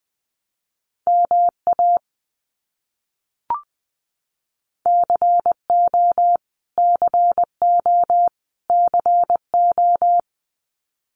Morse code, heard as a keyed tone, about 20 wpm, 700 Hz. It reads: MA CO CO CO
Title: US States Repeated 3x 20wpm